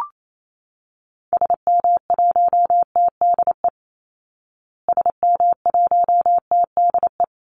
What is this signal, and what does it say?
Morse code, heard as a keyed tone, about 28 wpm, 700 Hz. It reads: SM1TDE SM1TDE
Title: Call Signs Repeated 2x - 28wpm